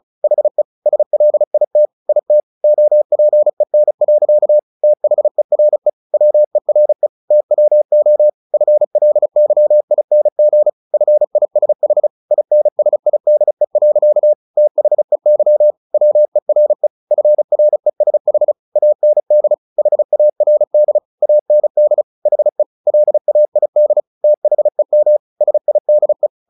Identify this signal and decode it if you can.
Morse code, heard as a keyed tone, about 34 wpm, 600 Hz. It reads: HE SLIT IT OPEN. THERE WERE TWO FLYING FISH INSIDE. THEY WERE FRESH AND HARD AND HE LAID THEM SIDE
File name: the old man and the sea 2 35wpm